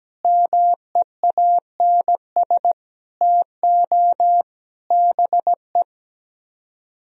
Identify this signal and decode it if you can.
Morse code, heard as a keyed tone, about 17 wpm, 700 Hz. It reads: MEANS TO BE
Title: Speed-Racing - Sets of 5 Words 17wpm